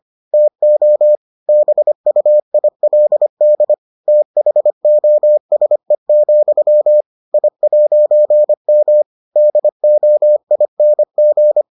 Morse code, heard as a keyed tone, about 25 wpm, 600 Hz. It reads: TO BUILD THOSE, I'M DOING